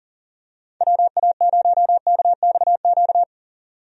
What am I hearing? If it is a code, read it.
Morse code, heard as a keyed tone, about 40 wpm, 700 Hz. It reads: WA0KXQ
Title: USA Call Signs - 6 Characters - 40wpm